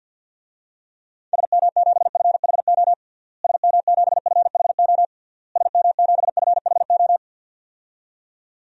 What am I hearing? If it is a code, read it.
Morse code, heard as a keyed tone, about 52 wpm, 700 Hz. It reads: SM7FHO SM7FHO SM7FHO
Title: DXCC-100 Call Signs - 6 Characters - Repeated 3x - 50wpm